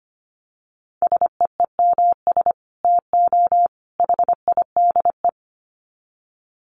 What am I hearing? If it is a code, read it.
Morse code, heard as a keyed tone, about 25 wpm, 700 Hz. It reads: SEEMS TO HIDE
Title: Dit Challenge 25wpm